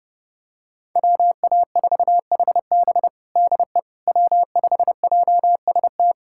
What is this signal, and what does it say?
Morse code, heard as a keyed tone, about 32 wpm, 700 Hz. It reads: WA4HB DE W5JST
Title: USA Callsign DE Callsign - 5 Characters - 30wpm